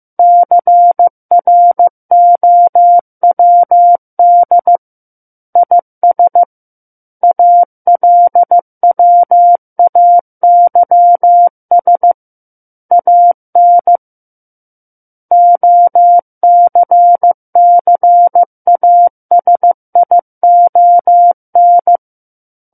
Morse code, heard as a keyed tone, about 15 wpm, 700 Hz. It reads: CROWD IS ALWAYS AN OCCASION
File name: Chapter 6 - 15 WPM